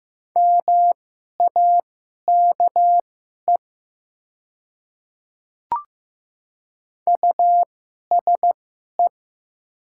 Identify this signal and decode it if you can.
Morse code, heard as a keyed tone, about 15 wpm, spaced as for 11 wpm, 690 Hz. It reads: MAKE USE
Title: 2x Character Spacing - Top 100 Words - 15wpm